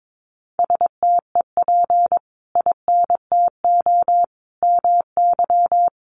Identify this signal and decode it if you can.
Morse code, heard as a keyed tone, about 22 wpm, 700 Hz. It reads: STEP INTO MY
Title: Sentences from Top 600 Words 22wpm